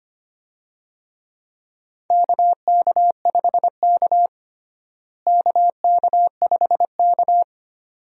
Morse code, heard as a keyed tone, about 25 wpm, 700 Hz. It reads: KK5K KK5K
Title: USA Call Signs - 4 Characters - Repeated 2x - 25wpm